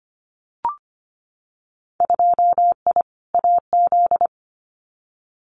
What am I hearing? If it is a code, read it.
Morse code, heard as a keyed tone, about 25 wpm, 700 Hz. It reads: 2I AZ
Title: Winter Field Day Exchanges Encore 25wpm